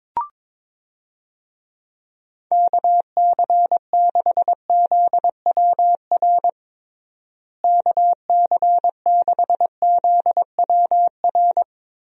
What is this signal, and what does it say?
Morse code, heard as a keyed tone, about 22 wpm, 700 Hz. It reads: KC6ZWR KC6ZWR
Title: USA Call Signs - 6 Characters - Repeated 3x - 22wpm